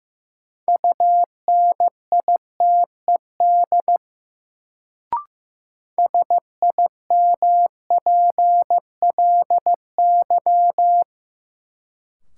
Morse code, heard as a keyed tone, about 15 wpm, 700 Hz. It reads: UNITED SIMPLY
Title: Sentences from Top 600 Words - New Word Drill - 15wpm